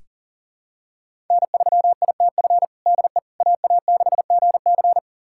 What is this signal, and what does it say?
Morse code, heard as a keyed tone, about 40 wpm, 700 Hz. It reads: N3ITF DE AA6GC
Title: USA Callsign DE Callsign - 5 Characters - 40wpm